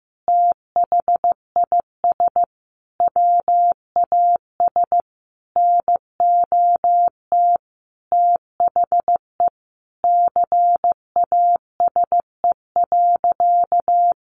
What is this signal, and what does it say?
Morse code, heard as a keyed tone, about 15 wpm, 700 Hz. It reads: THIS WAS NOT THE CASE.
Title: Sentences from Top 1200 Words 15wpm